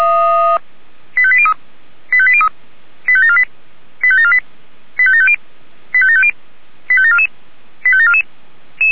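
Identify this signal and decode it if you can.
Five-tone selective call (ZVEI), heard as quick runs of five tones, tones of about 70 ms. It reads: 85803 85803 85748 85748 85750 85750 85744 85744
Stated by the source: ZVEI five-tone selective call received on SDR.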